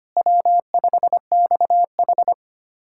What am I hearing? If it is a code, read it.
Morse code, heard as a keyed tone, about 25 wpm, 700 Hz. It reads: W5XH
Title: USA Callsign DE Callsign - 4 Characters - 25wpm